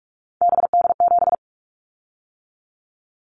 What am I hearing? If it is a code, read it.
Morse code, heard as a keyed tone, about 46 wpm, 700 Hz. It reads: BD7
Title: Callsign Prefixes for Top 100 DXCC - 45wpm